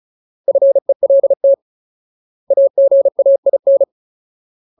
Morse code, heard as a keyed tone, about 34 wpm, 550 Hz. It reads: FELT AGAIN
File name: Ultimate Word List / 35WPM Ultimate